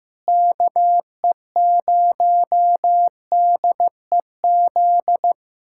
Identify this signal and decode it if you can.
Morse code, heard as a keyed tone, about 15 wpm, 700 Hz. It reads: KE0DEZ
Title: USA Callsign DE Callsign - 6 Characters - 15wpm